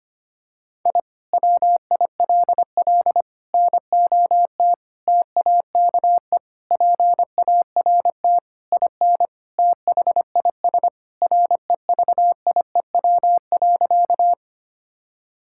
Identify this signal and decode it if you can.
Morse code, heard as a keyed tone, about 25 wpm, 700 Hz. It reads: I WILL NOT TAKE PART IN THIS REVIEW.
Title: Sentences from Top 1000 Words - Review Part 6 - 25wpm